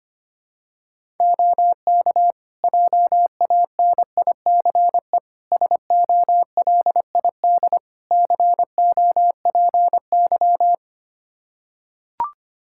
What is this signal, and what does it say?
Morse code, heard as a keyed tone, about 25 wpm, 700 Hz. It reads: OK JANICE SOLID COPY
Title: K1USN SST QSO Practice 25wpm